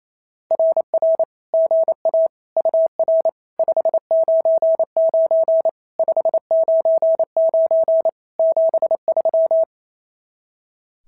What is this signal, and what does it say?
Morse code, heard as a keyed tone, about 24 wpm, 660 Hz. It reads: RR GA UR 599 599 73
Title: SOTA QSO Practice 28wpm